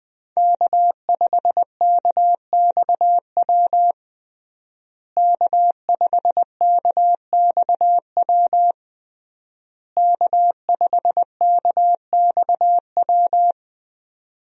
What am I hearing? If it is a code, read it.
Morse code, heard as a keyed tone, about 20 wpm, 700 Hz. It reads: K5KXW K5KXW K5KXW
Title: USA Call Signs - 5 Characters - Repeated 3x - 20wpm